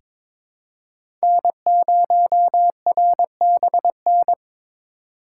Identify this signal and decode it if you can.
Morse code, heard as a keyed tone, about 22 wpm, 700 Hz. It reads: N0RBN
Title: USA Call Signs - 5 Characters - 4x Character Spacing - 22wpm